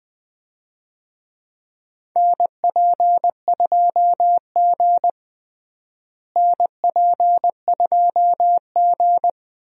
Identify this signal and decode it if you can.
Morse code, heard as a keyed tone, about 20 wpm, 700 Hz. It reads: NP2G NP2G
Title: USA Call Signs - 4 Characters - Repeated 3x - 20wpm